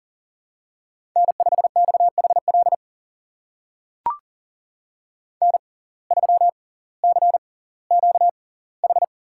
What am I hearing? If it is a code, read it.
Morse code, heard as a keyed tone, about 40 wpm, spaced as for 21 wpm, 690 Hz. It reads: N5XHL N3CQH
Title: USA Call Signs - 5 Characters - 6x Character Spacing - 40wpm